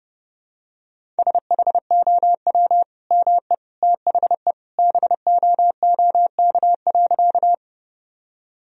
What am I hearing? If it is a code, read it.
Morse code, heard as a keyed tone, about 30 wpm, 700 Hz. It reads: SHOW ME THE BOOK.
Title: Sentences from Top 1000 Words - Review Part 1 - 30wpm